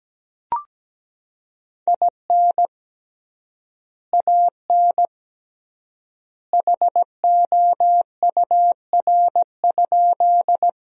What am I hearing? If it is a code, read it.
Morse code, heard as a keyed tone, about 17 wpm, 700 Hz. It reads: IN AN HOUR?
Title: 3x Word-Spacing - Sentences from Top 400 Words - 17wpm